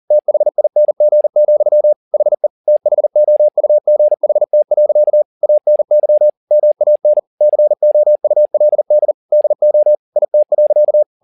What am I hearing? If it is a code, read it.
Morse code, heard as a keyed tone, about 40 wpm, 600 Hz. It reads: THING, HE THOUGHT. ANY MAN COULD DO IT.
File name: the old man and the sea 2 40wpm